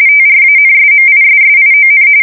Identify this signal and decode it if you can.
RTTY, about 45.45 baud. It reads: TRANSMISSION
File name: fsk170 / printing-telegraph